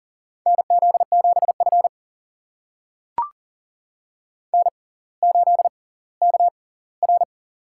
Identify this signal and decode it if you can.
Morse code, heard as a keyed tone, about 40 wpm, spaced as for 20 wpm, 700 Hz. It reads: NZ7F N8KR
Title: USA Call Signs - 4 Characters - 6x Character Spacing - 40wpm